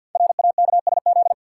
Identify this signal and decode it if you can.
Morse code, heard as a keyed tone, about 50 wpm, 700 Hz. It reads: RAKSZ
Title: Code Groups - 5 Letter:Numbers - 50wpm